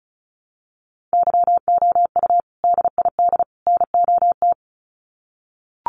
Morse code, heard as a keyed tone, about 34 wpm, 700 Hz. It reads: YOU DID NOT
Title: Sets of 3 Words 35wpm